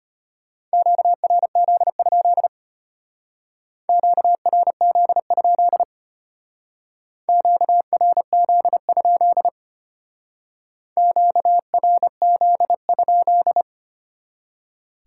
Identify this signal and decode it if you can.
Morse code, heard as a keyed tone, about 32 wpm, 700 Hz. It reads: QRZ? QRZ? QRZ? QRZ?